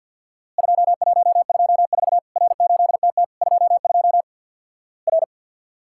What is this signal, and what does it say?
Morse code, heard as a keyed tone, about 51 wpm, 700 Hz. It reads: 2124 R8TT 22 R
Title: Worked All Europe DX Contest - QTC Exchange with Repeats - 50wpm